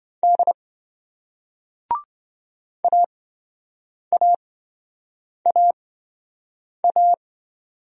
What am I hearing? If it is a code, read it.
Morse code, heard as a keyed tone, about 27 wpm, spaced as for 7 wpm, 700 Hz. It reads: D AAAA